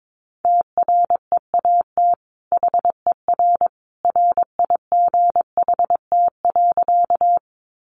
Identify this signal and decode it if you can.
Morse code, heard as a keyed tone, about 22 wpm, 700 Hz. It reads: TREAT HER RIGHT.
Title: Sentences from Top 1300 Words 22wpm